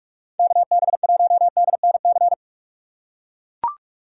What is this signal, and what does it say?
Morse code, heard as a keyed tone, about 41 wpm, 700 Hz. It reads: KB1DNC
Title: USA Call Signs - 2x Character Spacing - 45wpm